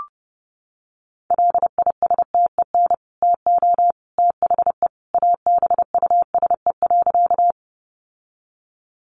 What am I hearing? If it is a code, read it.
Morse code, heard as a keyed tone, about 30 wpm, 700 Hz. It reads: LISTEN TO THE ABUSE.
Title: Sentences from Top 1300 Words 30wpm